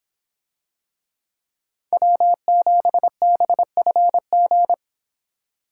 Morse code, heard as a keyed tone, about 26 wpm, 700 Hz. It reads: W7BFG